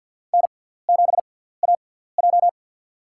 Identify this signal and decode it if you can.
Morse code, heard as a keyed tone, about 50 wpm, spaced as for 17 wpm, 690 Hz. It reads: N7AJ
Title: USA Call Signs - 4 Characters - 6x Character Spacing - 50wpm